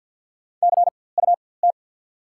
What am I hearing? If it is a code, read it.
Morse code, heard as a keyed tone, about 54 wpm, spaced as for 24 wpm, 700 Hz. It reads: CUT
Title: 4x Character Spacing - Top 500 Words - 50wpm